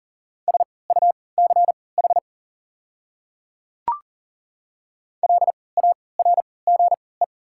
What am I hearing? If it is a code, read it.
Morse code, heard as a keyed tone, about 40 wpm, spaced as for 23 wpm, 690 Hz. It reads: SUCH LARGE
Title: 3x Character Spacing - Top 200 Words - 40wpm